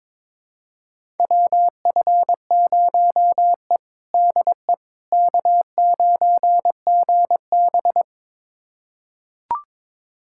Morse code, heard as a keyed tone, about 22 wpm, 700 Hz. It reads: WF0E DE K9GB
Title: USA Callsign DE Callsign - 4 Characters - 22wpm